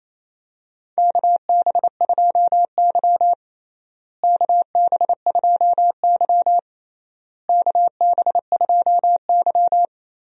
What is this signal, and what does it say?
Morse code, heard as a keyed tone, about 27 wpm, 700 Hz. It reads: KB2Y KB2Y KB2Y
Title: USA Call Signs - 4 Characters - Repeated 3x - 28wpm